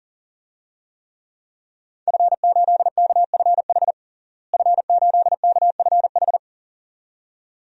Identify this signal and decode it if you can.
Morse code, heard as a keyed tone, about 37 wpm, 700 Hz. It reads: F8KFH F8KFH
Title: DXCC-100 Call Signs - 5 Characters - Repeated 2x - 40wpm